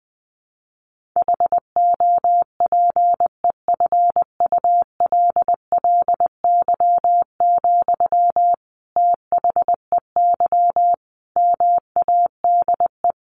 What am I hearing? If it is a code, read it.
Morse code, heard as a keyed tone, about 20 wpm, 700 Hz. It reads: HOPEFULLY, THEY MADE
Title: Sentences from Top 1500 Words 20wpm